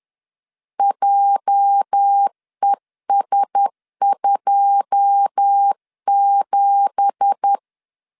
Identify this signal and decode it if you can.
Morse code, heard as a keyed tone, about 10 wpm, 800 Hz. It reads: JES27
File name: LEZIONI / 9 2867 10wpm 2733